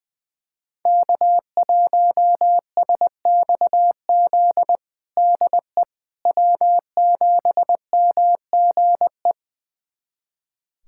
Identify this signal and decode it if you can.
Morse code, heard as a keyed tone, about 20 wpm, 700 Hz. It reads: K1SXZ DE W7MGE
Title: USA Callsign DE Callsign - 5 Characters - 20wpm